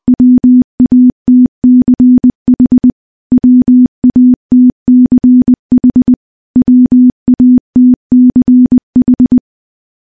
Morse code, heard as a keyed tone, about 20 wpm, 250 Hz. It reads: WATCH WATCH WATCH
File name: morse-training-20WPM